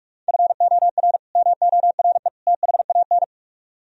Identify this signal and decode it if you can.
Morse code, heard as a keyed tone, about 66 wpm, spaced as for 55 wpm, 700 Hz. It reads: FOR MORE THAN
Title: Speed-Racing - Sets of 3 Words 45wpm